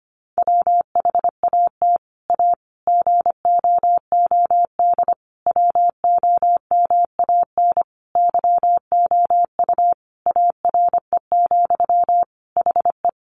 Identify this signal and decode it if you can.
Morse code, heard as a keyed tone, about 25 wpm, 700 Hz. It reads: WHAT A GOOD WOMAN YOU ARE, HE
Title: Sentences from Top 1100 Words 25wpm